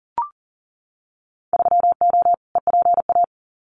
Morse code, heard as a keyed tone, about 40 wpm, 700 Hz. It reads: 3O EPA
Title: Winter Field Day Exchanges 40wpm